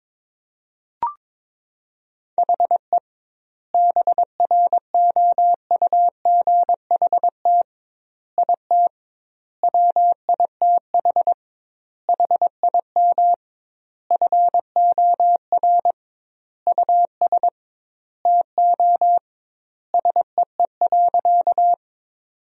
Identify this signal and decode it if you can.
Morse code, heard as a keyed tone, about 22 wpm, 700 Hz. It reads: HE BROUGHT IT WITH HIM FOR US TO SEE.
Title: 2x Word-Spacing - Sentences from Top 500 Words - 22wpm